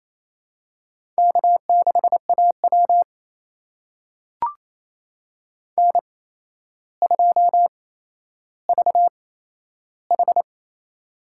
Morse code, heard as a keyed tone, about 27 wpm, spaced as for 12 wpm, 700 Hz. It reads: K6AW N2VH
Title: USA Call Signs - 4 Characters - 8x Character Spacing - 28wpm